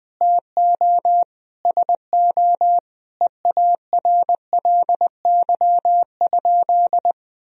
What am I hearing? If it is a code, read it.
Morse code, heard as a keyed tone, about 20 wpm, 700 Hz. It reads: TO SO EARLY?
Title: Sentences from Top 400 Words and Abbreviations 20wpm